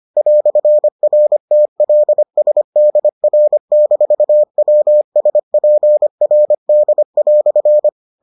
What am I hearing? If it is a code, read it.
Morse code, heard as a keyed tone, about 25 wpm, 600 Hz. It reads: "RTLSDR-WSPRD"